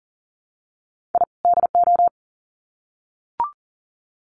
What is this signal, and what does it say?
Morse code, heard as a keyed tone, about 39 wpm, 700 Hz. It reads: I DO
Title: Sets of 2 Words 40wpm